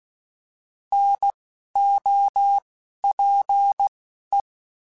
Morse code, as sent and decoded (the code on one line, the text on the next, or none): -. / --- / .--. / .
N O P E